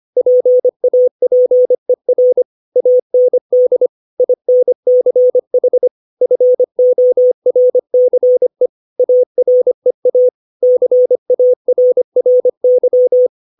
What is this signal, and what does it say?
Morse code, heard as a keyed tone, about 25 wpm, 500 Hz. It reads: PAPER AND INCH FORCE AREA CARRY
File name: 632 words / 632 Words 25WPM